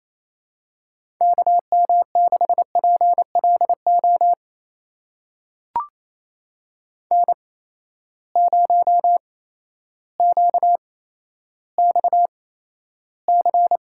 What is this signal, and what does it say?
Morse code, heard as a keyed tone, about 28 wpm, spaced as for 13 wpm, 700 Hz. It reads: KM6PLO N0QXC
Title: USA Call Signs - 8x Character Spacing - 28wpm